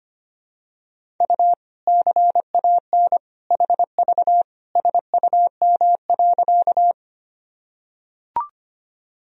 Morse code, heard as a keyed tone, about 25 wpm, 700 Hz. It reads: U CAN HV SUM.
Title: Sentences from Top 100 Words and Abbreviations 25wpm